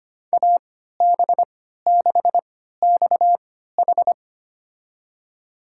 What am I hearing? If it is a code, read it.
Morse code, heard as a keyed tone, about 25 wpm, spaced as for 14 wpm, 700 Hz. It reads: AB6XH
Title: USA Call Signs - 5 Characters - 3x Character Spacing - 25wpm